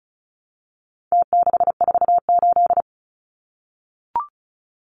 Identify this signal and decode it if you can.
Morse code, heard as a keyed tone, about 34 wpm, 700 Hz. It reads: T648
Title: Numbers - Contest Style - 4-Digit 35wpm